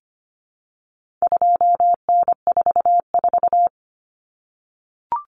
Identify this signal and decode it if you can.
Morse code, heard as a keyed tone, about 25 wpm, 700 Hz. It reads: 2N44
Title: Common Cut Numbers - 4-Digit 25wpm